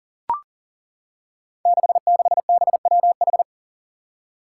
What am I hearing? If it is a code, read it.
Morse code, heard as a keyed tone, about 42 wpm, 700 Hz. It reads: 66BWH
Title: Five Challenging Letter-Number Combinations 40wpm